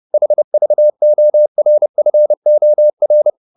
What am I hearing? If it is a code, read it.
Morse code, heard as a keyed tone, about 30 wpm, 600 Hz. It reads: HVORFOR
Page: Ferie 30 wpm